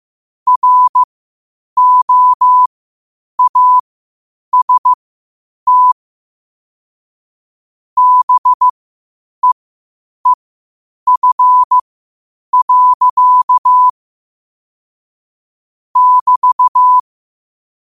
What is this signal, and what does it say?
Morse code, heard as a keyed tone, about 15 wpm, spaced as for 8 wpm, 1000 Hz. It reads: ROAST BEEF. <BT>
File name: qotdfiles / 8wpm